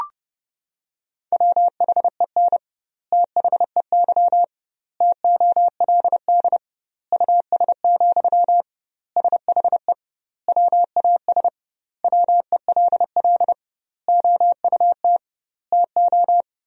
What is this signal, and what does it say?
Morse code, heard as a keyed tone, about 30 wpm, 700 Hz. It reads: WHEN THEY TOLD US, SHE WAS WELL OUT TO
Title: 2x Word-Spacing - Sentences from Top 400 Words - 30wpm